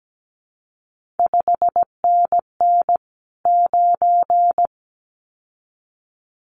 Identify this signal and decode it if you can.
Morse code, heard as a keyed tone, about 17 wpm, 700 Hz. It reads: 5NN 9
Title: CQ WW DX Contest Exchange Encore - 17wpm